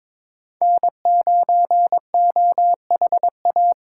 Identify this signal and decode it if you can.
Morse code, heard as a keyed tone, about 26 wpm, 700 Hz. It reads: N9OHA
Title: Speed Racing - USA 1x3 Call Signs 22wpm